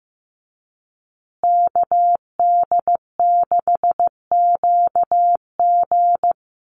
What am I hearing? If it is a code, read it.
Morse code, heard as a keyed tone, about 15 wpm, 700 Hz. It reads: KD6QG
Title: USA Call Signs Repeated 3x 15wpm